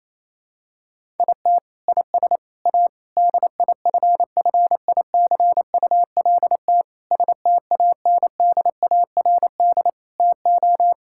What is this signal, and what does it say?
Morse code, heard as a keyed tone, about 28 wpm, 700 Hz. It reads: IT IS A DIFFICULT STANDARD TO
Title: Sentences from Top 1000 Words - Review Part 6 - 28wpm